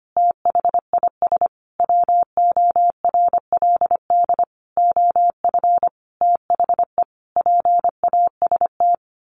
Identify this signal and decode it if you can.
Morse code, heard as a keyed tone, about 25 wpm, 700 Hz. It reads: THIS WORLD OF THE PAST
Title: A Sound Of Thunder 25wpm